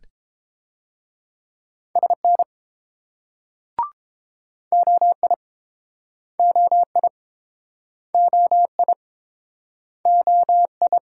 Morse code, heard as a keyed tone, about 30 wpm, 700 Hz. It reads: SN OI OI OI OI